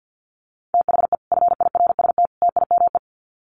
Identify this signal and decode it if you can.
Morse code, heard as a keyed tone, about 48 wpm, 700 Hz. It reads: THE FIRST TIME
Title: Sets of 3 Words 50wpm